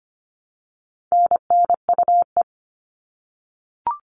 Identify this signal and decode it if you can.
Morse code, heard as a keyed tone, about 25 wpm, 700 Hz. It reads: NNUE
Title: All Cut Numbers - 4-Digit 25wpm